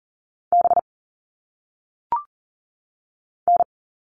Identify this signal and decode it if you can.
Morse code, heard as a keyed tone, about 40 wpm, 700 Hz. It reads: B N
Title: All Cut Numbers - 1-Digit 40wpm